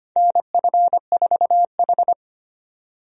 Morse code, heard as a keyed tone, about 25 wpm, 700 Hz. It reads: NF4H
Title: USA Call Signs - 4 Characters - 25wpm